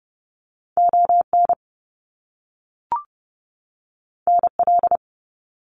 Morse code, heard as a keyed tone, about 30 wpm, 700 Hz. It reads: ON NL
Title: Canadian Provinces and Territories 30wpm